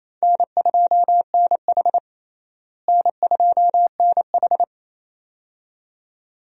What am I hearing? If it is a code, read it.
Morse code, heard as a keyed tone, about 28 wpm, 700 Hz. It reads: N2NH N2NH
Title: USA Call Signs - 4 Characters - Repeated 2x - 28wpm